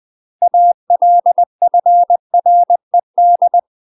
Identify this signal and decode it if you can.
Morse code, heard as a keyed tone, about 20 wpm, 700 Hz. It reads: ALFRED
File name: UK MASTER QSOs TRAINING / 20WPM / UK MASTER QSOs 0053